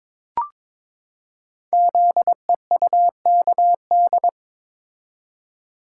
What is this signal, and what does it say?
Morse code, heard as a keyed tone, about 22 wpm, 700 Hz. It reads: ZEUKD
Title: Code Groups - 5 Letters - 22wpm